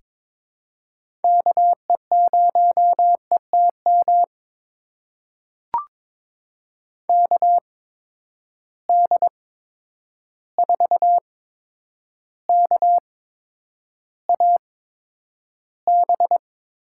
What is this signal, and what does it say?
Morse code, heard as a keyed tone, about 22 wpm, spaced as for 9 wpm, 700 Hz. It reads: KE0ETM KD4KAB
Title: USA Call Signs - 6 Characters - 8x Character Spacing - 22wpm